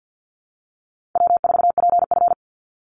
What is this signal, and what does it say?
Morse code, heard as a keyed tone, about 50 wpm, 700 Hz. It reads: W4PF
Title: Easy Call Signs - 50wpm